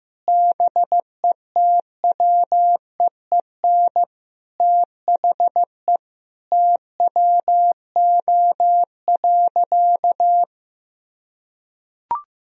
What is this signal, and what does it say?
Morse code, heard as a keyed tone, about 15 wpm, 700 Hz. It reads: BETWEEN THE TWO.
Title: Sentences from Top 1000 Words - Review Part 4 - 15wpm